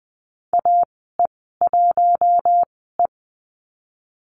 Morse code, heard as a keyed tone, about 20 wpm, spaced as for 14 wpm, 700 Hz. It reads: AE1E
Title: 042 Lesson 8g - Number 1 - Call Signs 20wpm